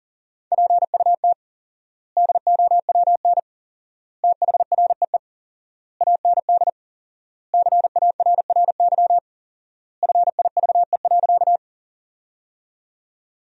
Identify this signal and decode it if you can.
Morse code, heard as a keyed tone, about 37 wpm, 700 Hz. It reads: PUT DOWN THREE AND CARRY FIVE.
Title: 4x Word-Spacing - Sentences from Top 400 Words - 40wpm